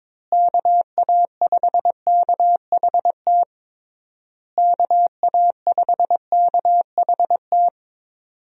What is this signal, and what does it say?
Morse code, heard as a keyed tone, about 22 wpm, 700 Hz. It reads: KA5KHT KA5KHT
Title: USA Call Signs - 6 Characters - Repeated 3x - 22wpm